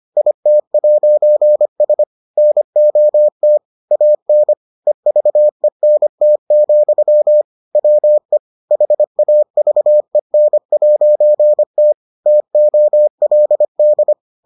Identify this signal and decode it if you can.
Morse code, heard as a keyed tone, about 25 wpm, 600 Hz. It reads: IT'S NOT AN EVENT, WE HAVEN'T TOLD